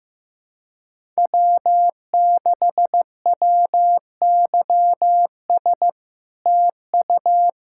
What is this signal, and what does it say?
Morse code, heard as a keyed tone, about 16 wpm, 700 Hz. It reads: W6WYS TU
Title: SOTA QSO Practice 15wpm